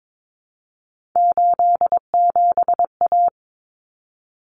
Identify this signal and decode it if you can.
Morse code, heard as a keyed tone, about 22 wpm, 700 Hz. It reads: 87A
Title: Common Cut Numbers - 3-Digit 22wpm